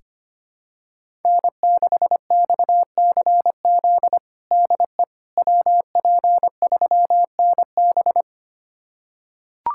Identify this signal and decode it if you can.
Morse code, heard as a keyed tone, about 25 wpm, 700 Hz. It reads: N6XCZ DE WP3NB
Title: USA Callsign DE Callsign - 5 Characters - 25wpm